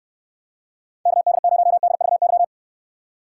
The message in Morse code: KD0DVY

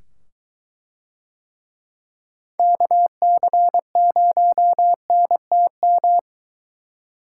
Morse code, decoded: KC0NTM

